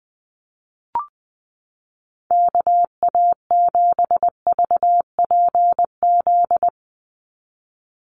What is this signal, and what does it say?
Morse code, heard as a keyed tone, about 20 wpm, 700 Hz. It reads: KA7VPZ
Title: USA Call Signs 20wpm